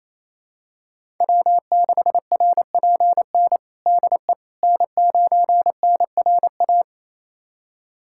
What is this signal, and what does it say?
Morse code, heard as a keyed tone, about 28 wpm, 700 Hz. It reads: W6RPN DE N9NRA
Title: USA Callsign DE Callsign - 5 Characters - 28wpm